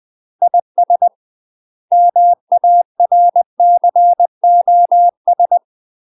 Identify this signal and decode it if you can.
Morse code, heard as a keyed tone, about 20 wpm, 700 Hz. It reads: IS MARCOS
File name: UK MASTER QSOs TRAINING/20WPM